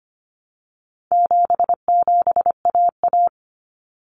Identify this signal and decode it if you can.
Morse code, heard as a keyed tone, about 25 wpm, 700 Hz. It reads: 77AA
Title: Common Cut Numbers - 4-Digit 25wpm